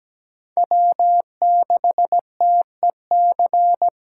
Morse code, heard as a keyed tone, about 17 wpm, 700 Hz. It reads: W6TEC
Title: USA Call Signs - 5 Characters - 2x Character Spacing - 17wpm